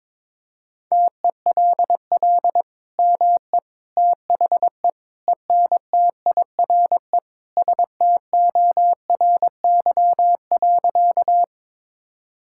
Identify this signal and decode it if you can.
Morse code, heard as a keyed tone, about 22 wpm, 700 Hz. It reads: TELL ME THE ENTIRE STORY.
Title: Sentences from Top 1000 Words - Review Part 5 - 22wpm